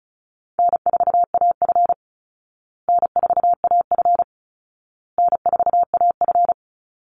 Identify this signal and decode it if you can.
Morse code, heard as a keyed tone, about 34 wpm, 700 Hz. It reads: N4AF N4AF N4AF
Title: Easy Call Signs Repeated 3x - 35wpm